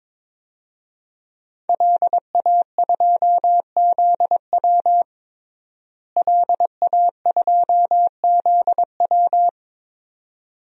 Morse code, heard as a keyed tone, about 21 wpm, 700 Hz. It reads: LA2ZW LA2ZW
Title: DXCC-100 Call Signs - 5 Characters - Repeated 2x - 22wpm